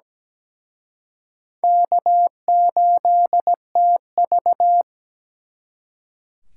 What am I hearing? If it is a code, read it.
Morse code, heard as a keyed tone, about 20 wpm, 700 Hz. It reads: K8TV
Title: Speed Racing - USA 1x2 Call Signs 17wpm